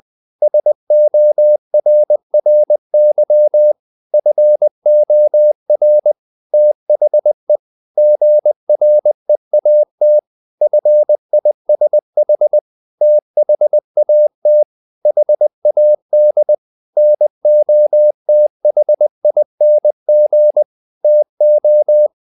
Morse code, heard as a keyed tone, about 20 wpm, 600 Hz. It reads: SORRY FOR THE GREAT FISH THAT HAD NOTHING TO
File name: the old man and the sea part 2 20wpm